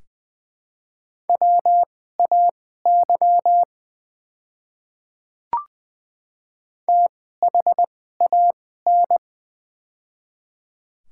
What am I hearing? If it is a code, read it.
Morse code, heard as a keyed tone, about 21 wpm, spaced as for 15 wpm, 700 Hz. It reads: WAY THAN
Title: 2x Character Spacing - Top 100 Words - 20wpm